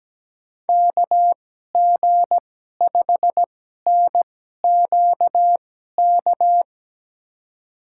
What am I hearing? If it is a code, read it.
Morse code, heard as a keyed tone, about 17 wpm, spaced as for 12 wpm, 700 Hz. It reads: KG5NQK
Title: USA Call Signs - 6 Characters - 2x Character Spacing - 17wpm